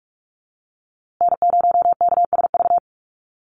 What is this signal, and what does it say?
Morse code, heard as a keyed tone, about 44 wpm, 700 Hz. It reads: N0KSV
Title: USA Call Signs Repeated 2x 45wpm